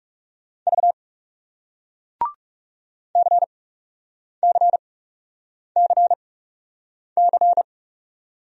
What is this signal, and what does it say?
Morse code, heard as a keyed tone, about 39 wpm, spaced as for 8 wpm, 700 Hz. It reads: V CCCC